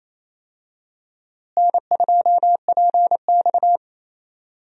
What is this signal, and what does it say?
Morse code, heard as a keyed tone, about 27 wpm, 700 Hz. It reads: N2PX
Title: USA Call Signs - 4 Characters - Repeated 3x - 28wpm